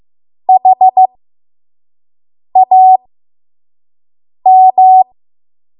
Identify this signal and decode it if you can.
Morse code, heard as a keyed tone, about 15 wpm, spaced as for 5 wpm, 750 Hz. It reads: HAM
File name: Morse / 05WPM